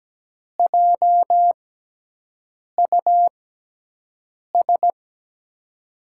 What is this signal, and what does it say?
Morse code, heard as a keyed tone, about 17 wpm, spaced as for 6 wpm, 700 Hz. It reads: JUS
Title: USA Call Signs - 6 Characters - 6x Character Spacing - 17wpm